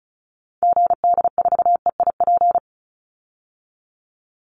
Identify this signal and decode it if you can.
Morse code, heard as a keyed tone, about 34 wpm, 700 Hz. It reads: GD4EIP
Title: Call Signs Repeated 3x - 35wpm